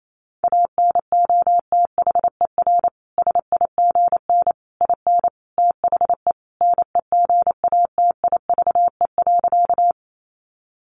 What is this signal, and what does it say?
Morse code, heard as a keyed tone, about 28 wpm, 700 Hz. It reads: ANOTHER SIGN IN THE NEGATIVE.
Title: Sentences from Top 1000 Words 28wpm